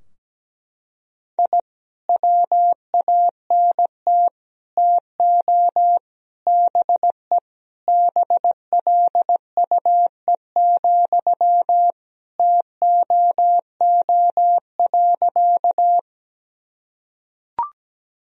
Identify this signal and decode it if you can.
Morse code, heard as a keyed tone, about 17 wpm, 700 Hz. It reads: I WANT TO BE BLUE, TOO.
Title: Sentences from Top 1000 Words - Review Part 3 - 17wpm